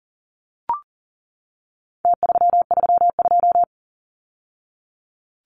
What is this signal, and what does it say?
Morse code, heard as a keyed tone, about 40 wpm, 700 Hz. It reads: T332
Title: Numbers - Contest Style - 4-Digit 40wpm